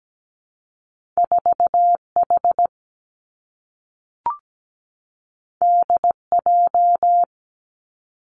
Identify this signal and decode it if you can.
Morse code, heard as a keyed tone, about 17 wpm, 700 Hz. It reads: 4H DJ
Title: Two Challenging Letter-Number Combinations 17wpm